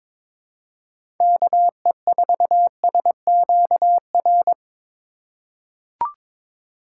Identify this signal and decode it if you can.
Morse code, heard as a keyed tone, about 22 wpm, 700 Hz. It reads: KE4SQR
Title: USA Call Signs - 6 Characters - 4x Character Spacing - 22wpm